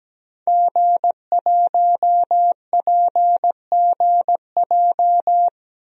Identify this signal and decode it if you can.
Morse code, heard as a keyed tone, about 17 wpm, 700 Hz. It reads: G1PGJ
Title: DXCC-100 Call Signs - 5 Characters - 17wpm